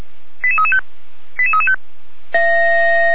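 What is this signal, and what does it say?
Five-tone selective call (ZVEI), heard as quick runs of five tones, tones of about 70 ms. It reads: 88335 88335
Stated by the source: ZVEI five-tone call 88335 received by SDR, archived by Feuerwehr Orlamünde (fire brigade).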